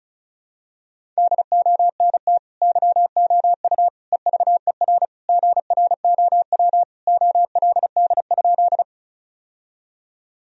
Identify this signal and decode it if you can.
Morse code, heard as a keyed tone, about 34 wpm, 700 Hz. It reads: DONT YOU EVER GROW OLD?
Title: Sentences from Top 1000 Words - Review Part 1 - 35wpm